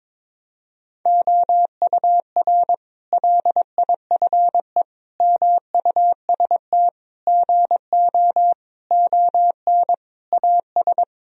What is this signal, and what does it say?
Morse code, heard as a keyed tone, about 22 wpm, 700 Hz. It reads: OUR LIFE MUST GO ON AS
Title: Sentences from Top 1000 Words - Review Part 3 - 22wpm